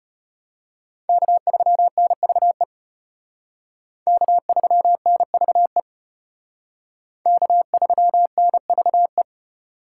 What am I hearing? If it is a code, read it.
Morse code, heard as a keyed tone, about 32 wpm, 700 Hz. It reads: K3NVE K3NVE K3NVE